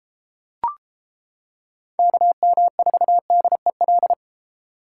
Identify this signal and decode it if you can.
Morse code, heard as a keyed tone, about 30 wpm, 700 Hz. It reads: KM4DEL